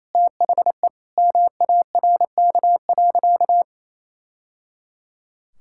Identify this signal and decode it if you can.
Morse code, heard as a keyed tone, about 27 wpm, 700 Hz. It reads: THE MARK.
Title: Sentences from Top 1000 Words - Review Part 6 - 28wpm